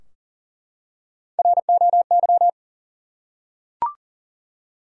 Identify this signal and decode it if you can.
Morse code, heard as a keyed tone, about 40 wpm, 700 Hz. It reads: ROY
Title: USA Common Names - Max 3 Letters - 40wpm